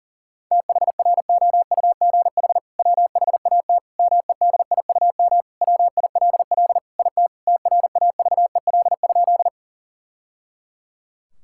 Morse code, heard as a keyed tone, about 40 wpm, 700 Hz. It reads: THROUGH WHAT MEDIUM WILL IT TRAVEL?
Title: Sentences from Top 1000 Words - Review Part 6 - 40wpm